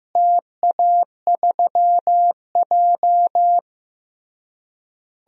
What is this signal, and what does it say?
Morse code, heard as a keyed tone, about 15 wpm, 700 Hz. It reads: TA3J
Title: DXCC-100 Call Signs - 4 Characters - Repeated 3x - 15wpm